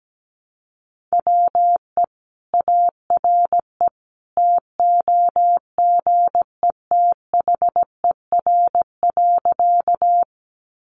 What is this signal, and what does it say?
Morse code, heard as a keyed tone, about 17 wpm, 700 Hz. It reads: WE ARE TOGETHER.
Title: Sentences from Top 300 Words 17wpm